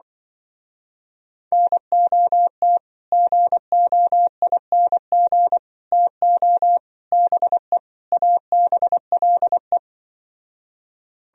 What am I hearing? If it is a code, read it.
Morse code, heard as a keyed tone, about 25 wpm, 700 Hz. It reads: NOT GOING TO BE ABLE